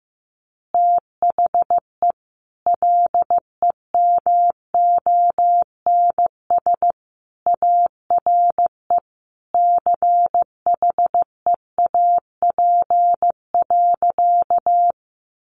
Morse code, heard as a keyed tone, about 15 wpm, 700 Hz. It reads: THE LEMONS ARE CHEAP.
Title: Additional Easy Phrases 15wpm